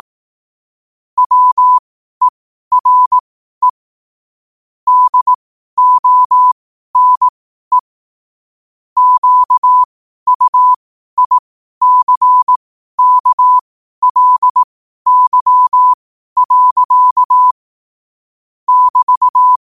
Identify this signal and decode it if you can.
Morse code, heard as a keyed tone, about 18 wpm, spaced as for 12 wpm, 1000 Hz. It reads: WERE DONE QUICKLY. <BT>
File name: qotdfiles / 12wpm